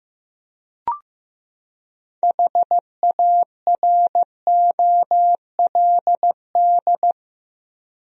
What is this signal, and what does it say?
Morse code, heard as a keyed tone, about 15 wpm, 700 Hz. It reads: HAROLD
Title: USA Common Names - 15wpm